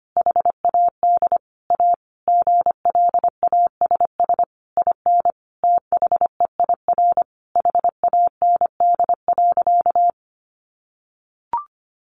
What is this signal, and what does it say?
Morse code, heard as a keyed tone, about 25 wpm, 700 Hz. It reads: HAD A GLASS IN THEIR HAND.
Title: Sentences from Top 1300 Words 25wpm